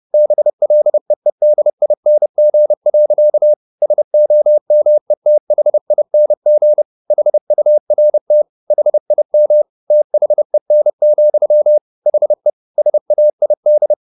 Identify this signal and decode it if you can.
Morse code, heard as a keyed tone, about 30 wpm, 600 Hz. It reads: BLEEDING. SOMETHING HURT HIM THEN, HE SAID
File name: the old man and the sea / the old man and the sea 1 30wpm